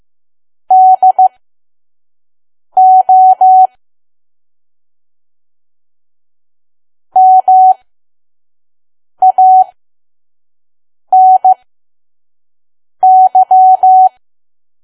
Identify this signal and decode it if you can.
Morse code, heard as a keyed tone, about 15 wpm, spaced as for 5 wpm, 750 Hz. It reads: DO MANY
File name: Archive / 5 WPM files / PM